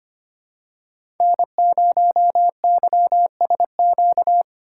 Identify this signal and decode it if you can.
Morse code, heard as a keyed tone, about 25 wpm, 700 Hz. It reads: N0YSQ
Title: K1USN SST QSO Practice - 25wpm